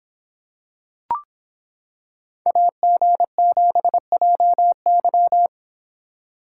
Morse code, heard as a keyed tone, about 26 wpm, 700 Hz. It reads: AG7JY